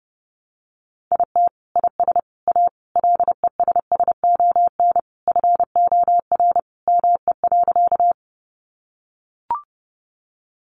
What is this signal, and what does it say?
Morse code, heard as a keyed tone, about 30 wpm, 700 Hz. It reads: IT IS A LESSON FOR ME.
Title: Sentences from Top 1200 Words 30wpm